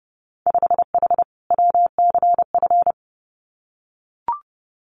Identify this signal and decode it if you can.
Morse code, heard as a keyed tone, about 30 wpm, 700 Hz. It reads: 5H WCF
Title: Winter Field Day Exchanges Encore 30wpm